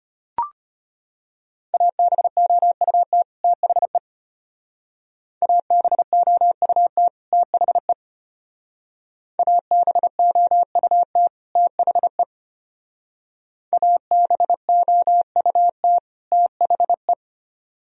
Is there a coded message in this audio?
Yes: Morse code from 1.7 to 17.1 s, heard as a keyed tone, about 33 wpm, 700 Hz. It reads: ABOUT THE ABOUT THE ABOUT THE ABOUT THE